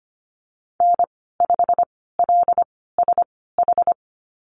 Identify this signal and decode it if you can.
Morse code, heard as a keyed tone, about 25 wpm, spaced as for 16 wpm, 700 Hz. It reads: N5LSH
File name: Lesson 15g - Number 5 - Call Signs 25wpm